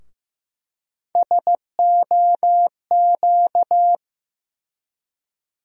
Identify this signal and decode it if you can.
Morse code, heard as a keyed tone, about 15 wpm, 700 Hz. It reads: SOQ
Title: USA Call Sign Postfixes - 3 Letters 15wpm